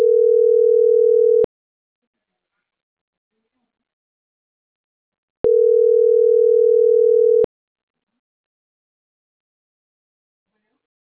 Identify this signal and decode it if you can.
Ringback tone (North America), 440 + 480 Hz.
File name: CENTRAL NAVIERAS / out onelinecapacitacion - .78748